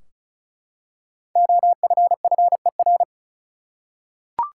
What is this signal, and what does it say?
Morse code, heard as a keyed tone, about 36 wpm, 700 Hz. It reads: OFFER
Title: Five Letter Words - 35wpm